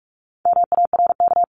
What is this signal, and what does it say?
Morse code, heard as a keyed tone, about 44 wpm, 700 Hz. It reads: MARK
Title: CWT Exchange 45wpm